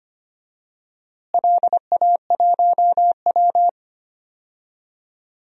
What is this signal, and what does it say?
Morse code, heard as a keyed tone, about 25 wpm, 700 Hz. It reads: LA1W
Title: DXCC-100 Call Signs - 4 Characters - Repeated 3x - 25wpm